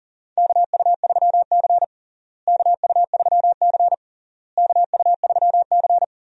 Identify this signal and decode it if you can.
Morse code, heard as a keyed tone, about 37 wpm, 700 Hz. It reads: KU3C KU3C KU3C
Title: USA Call Signs - 4 Characters - Repeated 3x - 40wpm